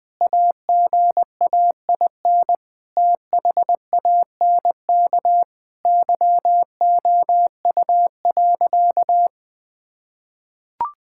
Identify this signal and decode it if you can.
Morse code, heard as a keyed tone, about 20 wpm, 700 Hz. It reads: AGAIN THANK YOU.
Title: Sentences with 8-Word Phrases - 20wpm